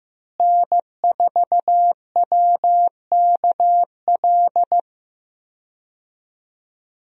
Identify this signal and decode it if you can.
Morse code, heard as a keyed tone, about 16 wpm, 700 Hz. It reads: N4WKL
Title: USA Call Signs - 5 Characters - Repeated 3x - 15wpm